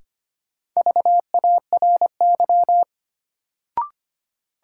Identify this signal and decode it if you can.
Morse code, heard as a keyed tone, about 25 wpm, 700 Hz. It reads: VARY
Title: Sentences from Top 1200 Words - New Word Drill Alt - 25wpm